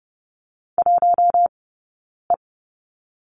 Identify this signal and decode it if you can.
Morse code, heard as a keyed tone, about 30 wpm, 700 Hz. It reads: 1 E
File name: 092 Lesson 16d - Letter c - Two Characters 30wpm